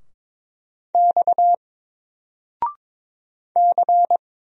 Morse code, heard as a keyed tone, about 22 wpm, 700 Hz. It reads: X C
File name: Single-Letter-Number-22wpm